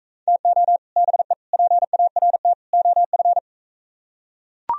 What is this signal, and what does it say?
Morse code, heard as a keyed tone, about 44 wpm, 700 Hz. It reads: TO BE PART OF